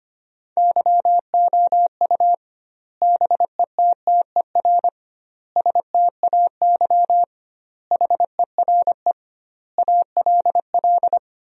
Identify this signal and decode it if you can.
Morse code, heard as a keyed tone, about 25 wpm, 700 Hz. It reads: YOU BETTER STAY HERE ALL
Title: 2x Word-Spacing - Sentences from Top 500 Words - 25wpm